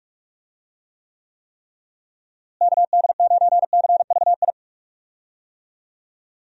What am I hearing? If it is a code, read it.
Morse code, heard as a keyed tone, about 46 wpm, 700 Hz. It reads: KD9CVI
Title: USA Call Signs - 6 Characters - Repeated 3x - 45wpm